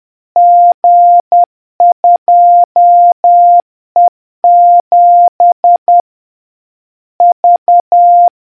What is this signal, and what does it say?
Morse code, heard as a keyed tone, about 10 wpm, 700 Hz. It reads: G2E7 V